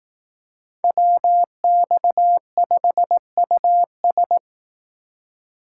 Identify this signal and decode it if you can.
Morse code, heard as a keyed tone, about 18 wpm, 700 Hz. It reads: WX5US
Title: Speed Racing - USA 2x2 Call Signs 15wpm